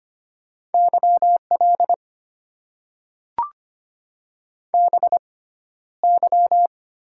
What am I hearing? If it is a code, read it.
Morse code, heard as a keyed tone, about 25 wpm, spaced as for 13 wpm, 700 Hz. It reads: YL BY
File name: Top 90 QSO Elements - 6x Character Spacing - 25wpm